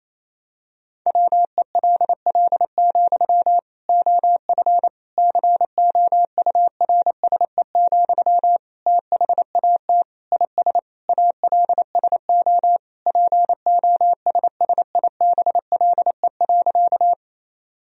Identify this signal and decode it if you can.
Morse code, heard as a keyed tone, about 28 wpm, 700 Hz. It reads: WELL, OF COURSE, THAT IS ALSO POSSIBLE.
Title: Sentences from Top 1000 Words - Review Part 1 - 28wpm